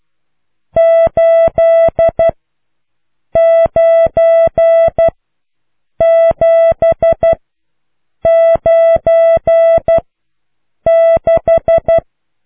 Morse code, heard as a keyed tone, about 12 wpm, spaced as for 7 wpm, 660 Hz. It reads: 89796